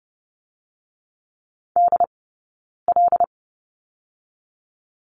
Morse code, heard as a keyed tone, about 30 wpm, 700 Hz. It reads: D L
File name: Lesson 19d - Letter w - Two Characters 30wpm